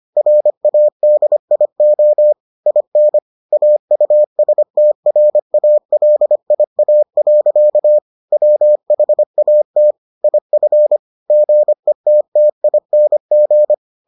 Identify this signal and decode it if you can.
Morse code, heard as a keyed tone, about 25 wpm, 600 Hz. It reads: RADIO IN AUSTRALIA. WHAT IF GETTING